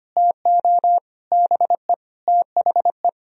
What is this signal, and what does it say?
Morse code, heard as a keyed tone, about 24 wpm, 690 Hz. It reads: TO BE THE
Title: Speed-Racing - Sets of 4 Words 25wpm